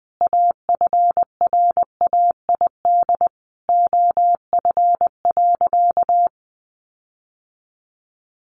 Morse code, heard as a keyed tone, about 20 wpm, 700 Hz. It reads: AFRAID OF.
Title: Sentences from Top 1500 Words 20wpm